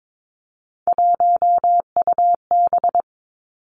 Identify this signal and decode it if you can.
Morse code, heard as a keyed tone, about 22 wpm, 700 Hz. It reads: 1UB
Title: Three Challenging Letter-Number Combinations 22wpm